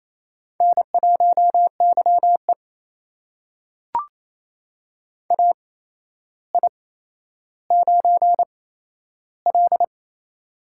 Morse code, heard as a keyed tone, about 27 wpm, spaced as for 12 wpm, 700 Hz. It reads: N1YE AI9L
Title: USA Call Signs - 4 Characters - 8x Character Spacing - 28wpm